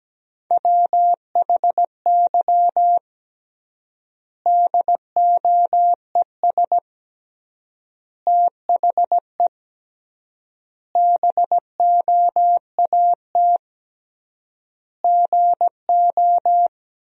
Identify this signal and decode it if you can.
Morse code, heard as a keyed tone, about 17 wpm, 700 Hz. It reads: WHY DOES THE BOAT GO
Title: 3x Word-Spacing - Sentences from Top 500 Words - 17wpm